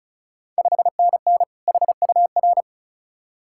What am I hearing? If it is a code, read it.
Morse code, heard as a keyed tone, about 36 wpm, 700 Hz. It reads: 5NN HUR
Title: Ontario QSO Party Exchange 35wpm